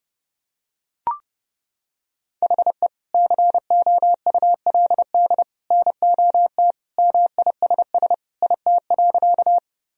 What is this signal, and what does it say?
Morse code, heard as a keyed tone, about 32 wpm, 700 Hz. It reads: HE COULD NOT MISS IT.
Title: Sentences from Top 1000 Words - Review Part 3 - 30wpm